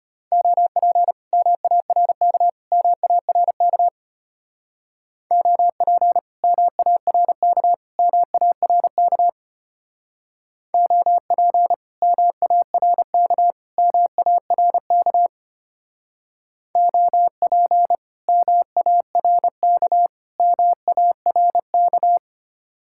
Morse code, heard as a keyed tone, about 30 wpm, 700 Hz. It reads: OP MARK MARK OP MARK MARK OP MARK MARK OP MARK MARK
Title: Speed-Racing Common QSO Phrases - 25wpm